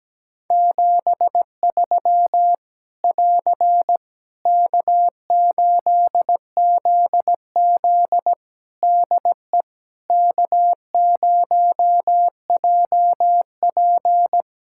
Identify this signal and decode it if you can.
Morse code, heard as a keyed tone, about 17 wpm, 700 Hz. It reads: 73 <AR> K8ZZ DE K0JP
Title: Practice QSO - Exchange 4 - Ending - 17wpm